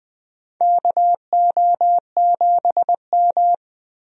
Morse code, heard as a keyed tone, about 20 wpm, 700 Hz. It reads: KO7M
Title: USA Call Signs - 4 Characters - 20wpm